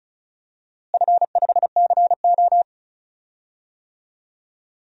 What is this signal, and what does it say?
Morse code, heard as a keyed tone, about 34 wpm, 700 Hz. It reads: F5CO
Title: DXCC-100 Call Signs - 4 Characters - Repeated 2x - 35wpm